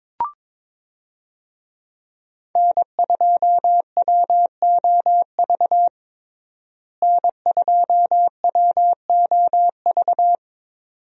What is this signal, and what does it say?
Morse code, heard as a keyed tone, about 22 wpm, 700 Hz. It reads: N2WOV N2WOV
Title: USA Call Signs - 5 Characters - Repeated 2x - 22wpm